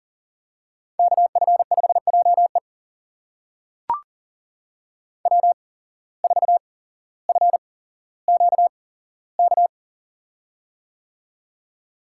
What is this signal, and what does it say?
Morse code, heard as a keyed tone, about 43 wpm, spaced as for 17 wpm, 700 Hz. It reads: KF5JE W4FQK
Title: USA Call Signs - 5 Characters - 8x Character Spacing - 40wpm